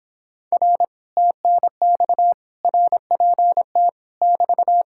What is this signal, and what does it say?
Morse code, heard as a keyed tone, about 26 wpm, 700 Hz. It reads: R TNX RPT <BT>